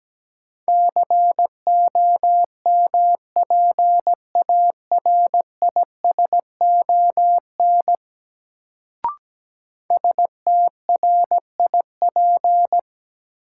Morse code, heard as a keyed tone, about 17 wpm, 700 Hz. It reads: COMPARISON STRIP
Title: Sentences from Top 1600 Words - New Word Drill - 17wpm